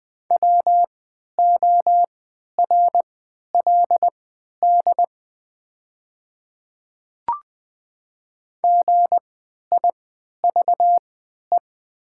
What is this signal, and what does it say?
Morse code, heard as a keyed tone, about 20 wpm, spaced as for 11 wpm, 700 Hz. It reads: WORLD GIVE
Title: 3x Character Spacing - Top 200 Words - 20wpm